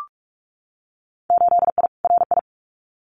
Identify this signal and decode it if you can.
Morse code, heard as a keyed tone, about 46 wpm, 700 Hz. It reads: 8I RI
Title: Winter Field Day Exchanges Encore 45wpm